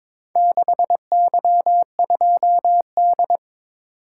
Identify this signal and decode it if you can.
Morse code, heard as a keyed tone, about 22 wpm, 700 Hz. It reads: 6Y2D
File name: practice / 22wpm / Four Letter-Number Combinations - 22wpm